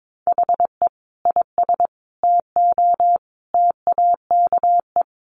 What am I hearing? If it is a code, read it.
Morse code, heard as a keyed tone, about 22 wpm, 700 Hz. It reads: HE IS TO TAKE
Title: Sentences from Top 1400 Words 22wpm